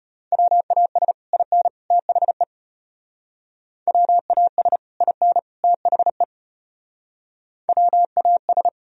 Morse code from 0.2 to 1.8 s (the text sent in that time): WAS IN